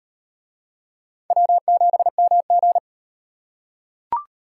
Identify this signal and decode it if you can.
Morse code, heard as a keyed tone, about 38 wpm, 700 Hz. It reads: W7MG